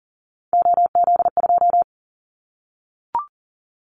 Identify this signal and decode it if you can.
Morse code, heard as a keyed tone, about 38 wpm, 700 Hz. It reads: OZ2
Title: Callsign Prefixes for Top 50 DXCC - 40wpm